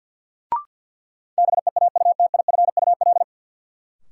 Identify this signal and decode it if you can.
Morse code, heard as a keyed tone, about 53 wpm, 700 Hz. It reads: BEAUTIFUL
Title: Sentences from Top 900 Words - New Word Drill - 50wpm